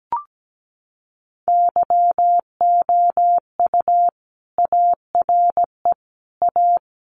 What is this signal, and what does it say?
Morse code, heard as a keyed tone, about 17 wpm, 700 Hz. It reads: YOU ARE A
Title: Sentences from Top 1100 Words 17wpm